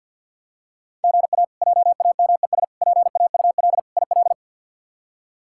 Morse code, heard as a keyed tone, about 47 wpm, 700 Hz. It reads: GA JAMES PAUL IL
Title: K1USN SST QSO Practice - 50wpm